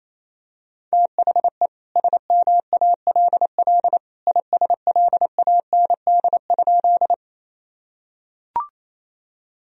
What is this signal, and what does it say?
Morse code, heard as a keyed tone, about 28 wpm, 700 Hz. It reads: THE SMALL ISLAND?
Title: Sentences from Top 1000 Words - Review Part 3 - 28wpm